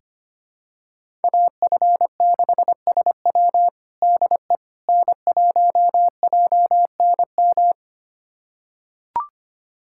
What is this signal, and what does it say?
Morse code, heard as a keyed tone, about 25 wpm, 700 Hz. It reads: AF6SW DE N1JNM
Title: USA Callsign DE Callsign - 5 Characters - 25wpm